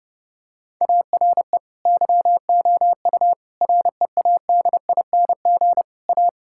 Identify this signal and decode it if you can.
Morse code, heard as a keyed tone, about 32 wpm, 700 Hz. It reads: ARE YOU READING A
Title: Sentences from Top 1000 Words - Review Part 6 - 30wpm